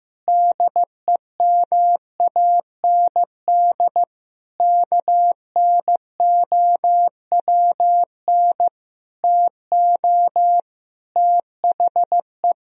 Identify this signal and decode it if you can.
Morse code, heard as a keyed tone, about 15 wpm, 700 Hz. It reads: DEMAND KNOWN TO THE
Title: Sentences from Top 1000 Words - Review Part 4 - 15wpm